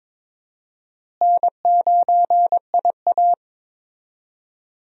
Morse code, heard as a keyed tone, about 22 wpm, 700 Hz. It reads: N9IA
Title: USA Call Signs - 4 Characters - Repeated 2x - 22wpm